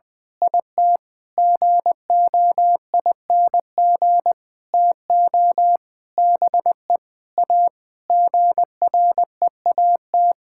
Morse code, heard as a keyed tone, about 20 wpm, 700 Hz. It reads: IT GOING TO BE A GREAT
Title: Sentences with 6-Word Phrases - 20wpm